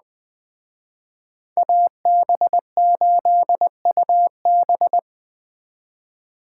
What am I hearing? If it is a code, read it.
Morse code, heard as a keyed tone, about 20 wpm, 700 Hz. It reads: AB8UB
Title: Speed Racing - USA Call Signs 17wpm